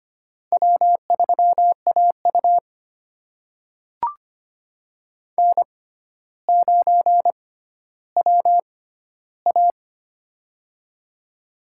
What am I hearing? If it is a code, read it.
Morse code, heard as a keyed tone, about 25 wpm, spaced as for 13 wpm, 700 Hz. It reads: W3AU N9WA
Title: USA Call Signs - 4 Characters - 6x Character Spacing - 25wpm